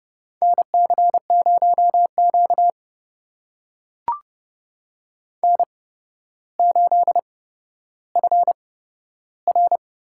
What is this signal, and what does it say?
Morse code, heard as a keyed tone, about 30 wpm, spaced as for 13 wpm, 700 Hz. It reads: NC0Q N8FR
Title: USA Call Signs - 4 Characters - 8x Character Spacing - 30wpm